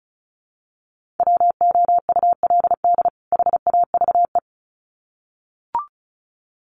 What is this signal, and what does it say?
Morse code, heard as a keyed tone, about 34 wpm, 700 Hz. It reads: WOULD HAVE
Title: Sets of 2 Words 35wpm